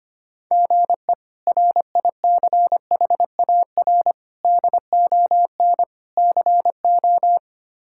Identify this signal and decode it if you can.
Morse code, heard as a keyed tone, about 25 wpm, 700 Hz. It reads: GE RICHAR DON CO
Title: K1USN SST QSO Practice - 25wpm